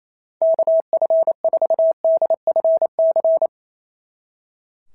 Morse code, heard as a keyed tone, about 27 wpm, 650 Hz. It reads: KF4DFC
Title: SOTA QSO Practice 28wpm